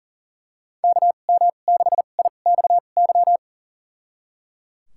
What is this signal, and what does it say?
Morse code, heard as a keyed tone, about 37 wpm, spaced as for 28 wpm, 700 Hz. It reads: KM6IXY
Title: USA Call Signs - 2x Character Spacing - 40wpm